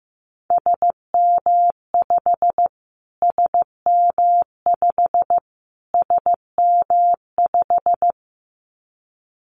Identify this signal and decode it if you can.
Morse code, heard as a keyed tone, about 15 wpm, 700 Hz. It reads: SM5 SM5 SM5
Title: Callsign Prefixes for Top 25 DXCC - Repeated 3x - 15wpm